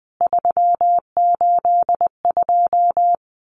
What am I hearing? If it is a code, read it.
Morse code, heard as a keyed tone, about 20 wpm, 700 Hz. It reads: 382
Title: CWT Exchange 20wpm